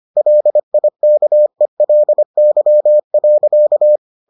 Morse code, heard as a keyed tone, about 25 wpm, 600 Hz. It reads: LIKELY.